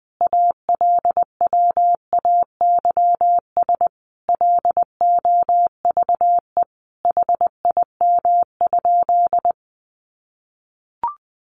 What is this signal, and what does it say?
Morse code, heard as a keyed tone, about 20 wpm, 700 Hz. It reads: ALWAYS LOVE HIM?
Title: Sentences from Top 400 Words 20wpm